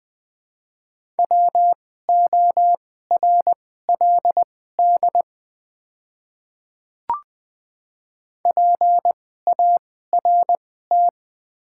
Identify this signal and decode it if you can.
Morse code, heard as a keyed tone, about 20 wpm, spaced as for 15 wpm, 700 Hz. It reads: WORLD PART
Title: 2x Character Spacing - Top 200 Words - 20wpm